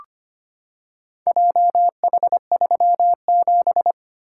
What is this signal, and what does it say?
Morse code, heard as a keyed tone, about 25 wpm, 700 Hz. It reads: JH37
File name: practice / 25wpm / Four Letter-Number Combinations - 25wpm